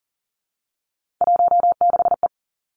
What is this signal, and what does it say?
Morse code, heard as a keyed tone, about 40 wpm, 700 Hz. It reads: 16E
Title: Three Challenging Letter-Number Combinations 40wpm